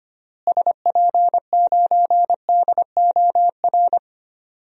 Morse code, heard as a keyed tone, about 25 wpm, 700 Hz. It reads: SP9DOR
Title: DXCC-100 Call Signs - 6 Characters - Repeated 3x - 25wpm